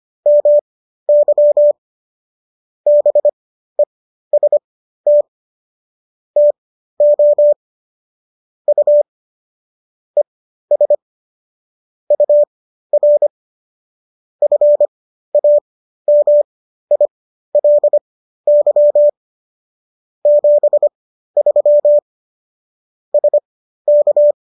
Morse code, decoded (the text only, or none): MY BEST TO U ES UR FAMILY 73 SK